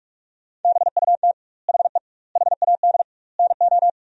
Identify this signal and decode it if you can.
Morse code, heard as a keyed tone, about 41 wpm, 700 Hz. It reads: BUT HE HAD NO
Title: 2x Word-Spacing - Sentences from Top 100 Words - 45wpm